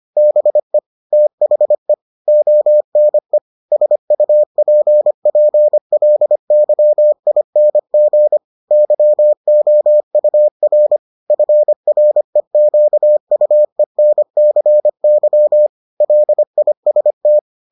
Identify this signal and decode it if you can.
Morse code, heard as a keyed tone, about 25 wpm, 600 Hz. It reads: BE THE ONE SUPPLYING YOUR FREQUENCY LIST